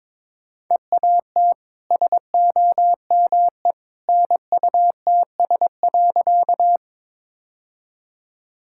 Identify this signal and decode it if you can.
Morse code, heard as a keyed tone, about 22 wpm, 690 Hz. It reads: EAT SOME NUTS.
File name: practice / 22wpm / Easy Sentences with Hints - 22wpm